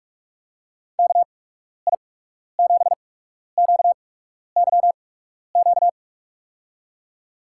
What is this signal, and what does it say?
Morse code, heard as a keyed tone, about 46 wpm, spaced as for 12 wpm, 700 Hz. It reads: KI7QYQ
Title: USA Call Signs - 6 Characters - 8x Character Spacing - 45wpm